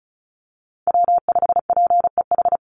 Morse code, heard as a keyed tone, about 34 wpm, 700 Hz. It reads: W5PEH
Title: Call Signs Repeated 2x - 35wpm